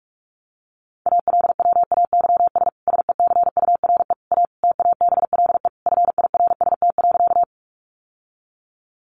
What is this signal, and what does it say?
Morse code, heard as a keyed tone, about 44 wpm, 700 Hz. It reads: ALWAYS SECURE A TABLE FIRST.
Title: Sentences from Top 1200 Words 45wpm